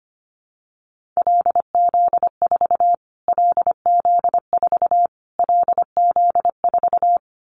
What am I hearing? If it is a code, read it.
Morse code, heard as a keyed tone, about 25 wpm, 700 Hz. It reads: LZ4 LZ4 LZ4
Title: Callsign Prefixes for Top 25 DXCC - Repeated 3x - 25wpm